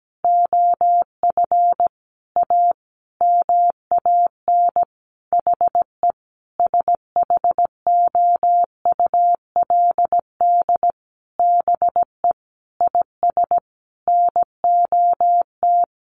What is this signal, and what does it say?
Morse code, heard as a keyed tone, about 17 wpm, 700 Hz. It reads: OF A MAN HE SHOULD BE IS NOT
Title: Sentences from Top 1300 Words 17wpm